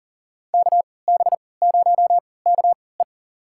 Morse code, heard as a keyed tone, about 40 wpm, spaced as for 23 wpm, 700 Hz. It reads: KB0KE
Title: USA Call Signs - 3x Character Spacing - 40wpm